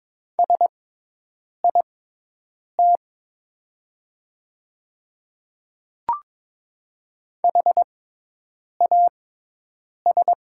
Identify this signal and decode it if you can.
Morse code, heard as a keyed tone, about 22 wpm, spaced as for 8 wpm, 700 Hz. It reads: SIT HAS
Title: 6x Character Spacing - Top 500 Words - 22wpm